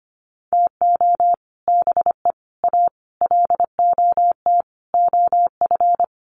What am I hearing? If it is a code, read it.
Morse code, heard as a keyed tone, about 25 wpm, 690 Hz. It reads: TO BE A LOT OF
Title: Sets of 5 Words Encore 25wpm